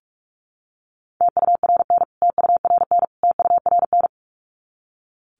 Morse code, heard as a keyed tone, about 44 wpm, 700 Hz. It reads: TURN TURN TURN
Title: Top 200 Words Repeated 45wpm